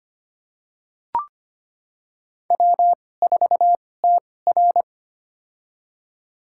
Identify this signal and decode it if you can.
Morse code, heard as a keyed tone, about 25 wpm, spaced as for 18 wpm, 700 Hz. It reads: W4TR
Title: USA Call Signs - 4 Characters - 2x Character Spacing - 25wpm